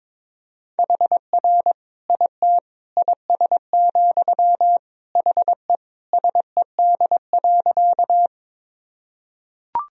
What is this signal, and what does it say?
Morse code, heard as a keyed tone, about 22 wpm, 700 Hz. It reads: HR IT IS, HE SED.
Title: Sentences from Top 200 Words and Abbreviations 22wpm